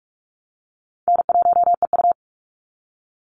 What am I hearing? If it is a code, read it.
Morse code, heard as a keyed tone, about 44 wpm, 700 Hz. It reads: N1EU
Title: NA SOTA Activator Callsigns 45wpm